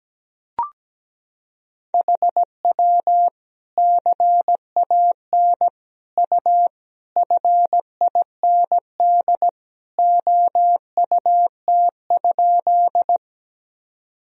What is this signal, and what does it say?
Morse code, heard as a keyed tone, about 17 wpm, 700 Hz. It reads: HW CAN U FIND OUT?
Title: Sentences from Top 200 Words and Abbreviations 17wpm